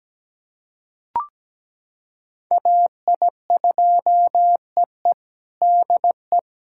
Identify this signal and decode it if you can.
Morse code, heard as a keyed tone, about 17 wpm, 700 Hz. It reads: AI2EE DE
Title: USA Callsign DE Callsign 17wpm